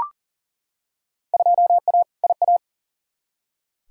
Morse code, heard as a keyed tone, about 40 wpm, 700 Hz. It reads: 2A IA
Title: Field Day Exchanges Encore - 40wpm